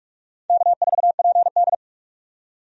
Morse code, heard as a keyed tone, about 46 wpm, 700 Hz. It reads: K4PD
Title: USA Call Signs - 4 Characters - 45wpm